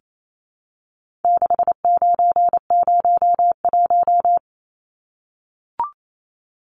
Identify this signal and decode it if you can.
Morse code, heard as a keyed tone, about 28 wpm, 700 Hz. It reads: 6901
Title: Numbers - 4-Digit 28wpm